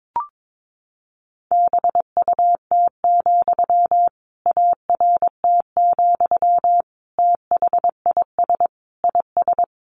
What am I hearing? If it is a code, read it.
Morse code, heard as a keyed tone, about 22 wpm, 700 Hz. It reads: BUT, ART, THIS IS
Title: Sentences from Top 600 Words 22wpm